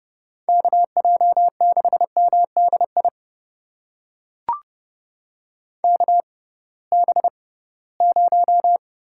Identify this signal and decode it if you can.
Morse code, heard as a keyed tone, about 32 wpm, spaced as for 20 wpm, 700 Hz. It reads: KJ6MDI KB0
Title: USA Call Signs - 6 Characters - 6x Character Spacing - 30wpm